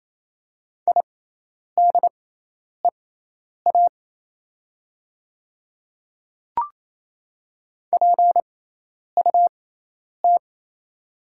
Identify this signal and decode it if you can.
Morse code, heard as a keyed tone, about 27 wpm, spaced as for 10 wpm, 700 Hz. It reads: IDEA PUT
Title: 6x Character Spacing - Top 300 Words - 28wpm